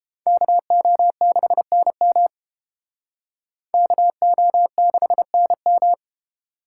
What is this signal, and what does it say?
Morse code, heard as a keyed tone, about 30 wpm, 700 Hz. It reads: KO6NM KO6NM